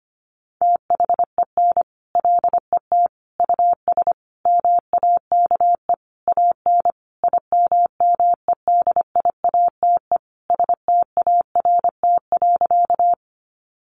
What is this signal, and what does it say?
Morse code, heard as a keyed tone, about 25 wpm, 700 Hz. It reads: THEN LET US MAKE AN IMMEDIATE START.
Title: Sentences from Top 1200 Words 25wpm